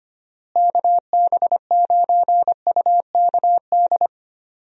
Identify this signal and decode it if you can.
Morse code, heard as a keyed tone, about 25 wpm, 700 Hz. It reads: KB9UKD
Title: USA Call Signs - 6 Characters - Repeated 3x - 25wpm